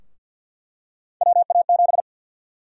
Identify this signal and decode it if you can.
Morse code, heard as a keyed tone, about 46 wpm, 700 Hz. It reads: WA7
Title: USA Call Sign Prefixes - Repeated 2x - 50wpm